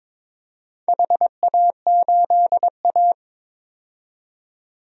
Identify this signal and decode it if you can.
Morse code, heard as a keyed tone, about 23 wpm, 700 Hz. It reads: HA8A
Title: DXCC-100 Call Signs - 4 Characters - 22wpm